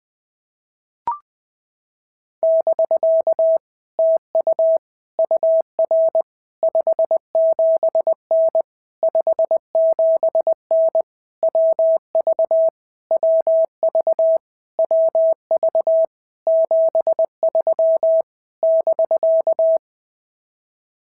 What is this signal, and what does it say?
Morse code, heard as a keyed tone, about 19 wpm, 650 Hz. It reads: <BK> TU UR 57N 57N WV WV WV 73 <BK>
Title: POTA QSO Practice - 20wpm